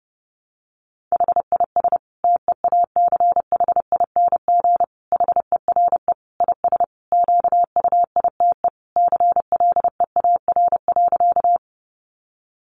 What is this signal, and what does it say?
Morse code, heard as a keyed tone, about 28 wpm, 700 Hz. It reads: HIS TEACHING HERE IS QUITE CLEAR.
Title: Sentences from Top 1100 Words 30wpm